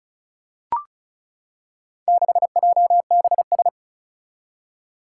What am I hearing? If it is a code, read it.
Morse code, heard as a keyed tone, about 36 wpm, 700 Hz. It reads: 6JBS
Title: Four Challenging Letter-Number Combinations 35wpm